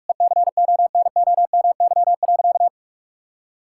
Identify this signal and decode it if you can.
Morse code, heard as a keyed tone, about 46 wpm, 700 Hz. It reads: ECONOMY.
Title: Sentences from Top 1000 Words - Review Part 5 - 45wpm